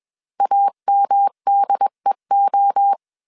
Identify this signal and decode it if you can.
Morse code, heard as a keyed tone, about 21 wpm, 800 Hz. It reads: AMDEO
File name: 20wpm